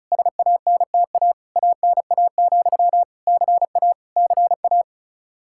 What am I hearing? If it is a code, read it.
Morse code, heard as a keyed tone, about 36 wpm, 700 Hz. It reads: SANTA ANA, CA CA
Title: Practice QSO - Exchange 1 - Beginning - 35wpm